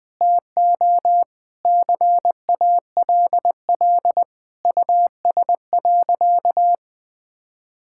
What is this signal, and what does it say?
Morse code, heard as a keyed tone, about 20 wpm, 700 Hz. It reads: TO CALL US.
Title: Sentences with 4-Word Phrases - 20wpm